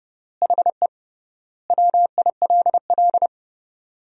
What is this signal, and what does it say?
Morse code, heard as a keyed tone, about 30 wpm, 700 Hz. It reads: HE WILL
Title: 3x Word-Spacing - Sentences from Top 300 Words - 30wpm